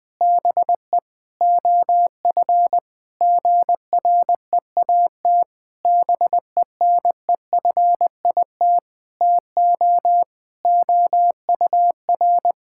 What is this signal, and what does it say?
Morse code, heard as a keyed tone, about 20 wpm, 700 Hz. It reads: BE OF GREAT BENEFIT TO OUR
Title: Sentences from Top 1000 Words - Review Part 6 - 20wpm